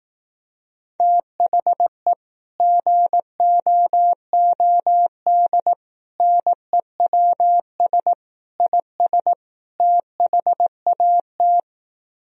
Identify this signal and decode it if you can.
Morse code, heard as a keyed tone, about 18 wpm, 700 Hz. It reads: THE GOOD NEWS IS THAT
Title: Speed-Racing - Sets of 5 Words 15wpm